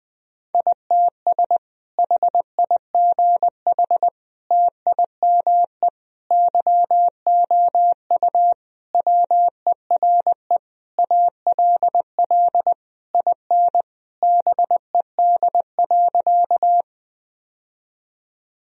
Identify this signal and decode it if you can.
Morse code, heard as a keyed tone, about 20 wpm, 700 Hz. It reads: ITS HIGH TIME YOU WERE ALL IN BED.
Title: Sentences from Top 1000 Words - Review Part 1 - 20wpm